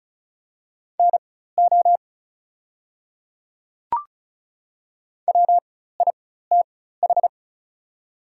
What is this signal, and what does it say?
Morse code, heard as a keyed tone, about 36 wpm, spaced as for 16 wpm, 690 Hz. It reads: NO WITH
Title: 4x Character Spacing - Top 100 Words - 35wpm